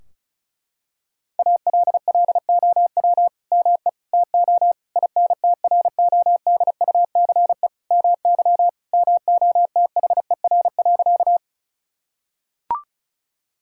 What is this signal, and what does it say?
Morse code, heard as a keyed tone, about 36 wpm, 700 Hz. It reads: ALLOW ME TO INTRODUCE MY MOTHER.
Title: Sentences from Top 1600 Words 35wpm